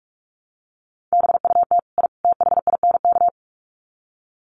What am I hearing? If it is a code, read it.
Morse code, heard as a keyed tone, about 44 wpm, 700 Hz. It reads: BUT I THINK
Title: Sets of 3 Words 45wpm